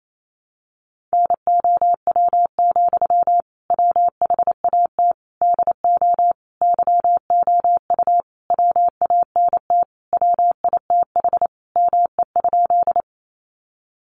Morse code, heard as a keyed tone, about 28 wpm, 700 Hz. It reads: NOW, WHAT DO YOU WANT WITH ME?
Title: Sentences from Top 200 Words 28wpm